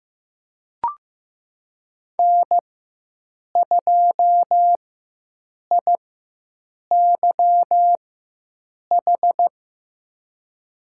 Morse code, heard as a keyed tone, about 15 wpm, spaced as for 7 wpm, 700 Hz. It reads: N2IYH
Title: USA Call Signs - 5 Characters - 4x Character Spacing - 15wpm